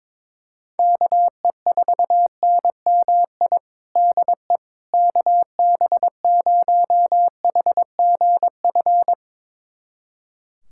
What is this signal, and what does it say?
Morse code, heard as a keyed tone, about 22 wpm, 700 Hz. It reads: KE4NMI DE KB0HGF
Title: USA Callsign DE Callsign - 6 Characters - 22wpm